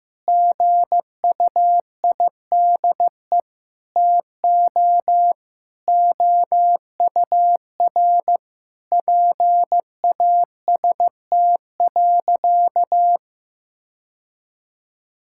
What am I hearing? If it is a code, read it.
Morse code, heard as a keyed tone, about 15 wpm, 700 Hz. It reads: GUIDE TO OUR PAST.
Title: Sentences from Top 1000 Words - Review Part 6 - 15wpm